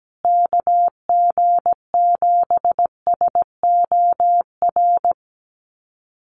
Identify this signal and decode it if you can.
Morse code, heard as a keyed tone, about 17 wpm, 700 Hz. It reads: KG7SOR
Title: USA Call Signs Repeated 2x 17wpm